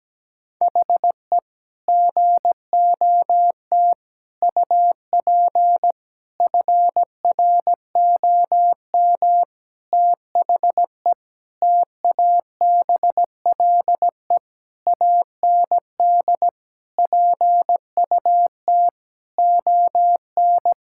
Morse code, heard as a keyed tone, about 17 wpm, 700 Hz. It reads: HE GOT UP FROM THE TABLE AND PUT ON
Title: Sentences from Top 1600 Words 17wpm